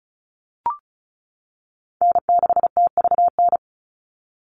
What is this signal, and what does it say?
Morse code, heard as a keyed tone, about 36 wpm, 700 Hz. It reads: N6TVN
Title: Call Signs - 35wpm